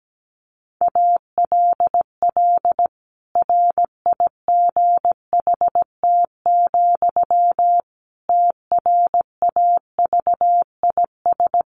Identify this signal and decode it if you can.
Morse code, heard as a keyed tone, about 17 wpm, 700 Hz. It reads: ALL RIGHT, TRAVIS
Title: A Sound Of Thunder 17wpm